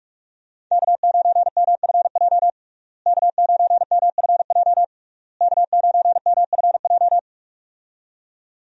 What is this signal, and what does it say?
Morse code, heard as a keyed tone, about 46 wpm, 700 Hz. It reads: K9MFJ K9MFJ K9MFJ
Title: USA Call Signs - 5 Characters - Repeated 3x - 45wpm